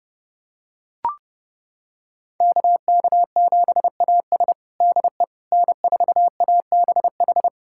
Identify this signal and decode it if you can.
Morse code, heard as a keyed tone, about 30 wpm, 700 Hz. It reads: KK7AS DE N4ABH
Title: USA Callsign DE Callsign - 5 Characters - 30wpm